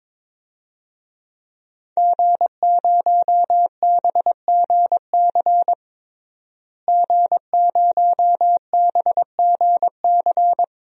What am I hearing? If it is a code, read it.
Morse code, heard as a keyed tone, about 22 wpm, 700 Hz. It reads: G0BGC G0BGC
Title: DXCC-100 Call Signs - 5 Characters - Repeated 2x - 22wpm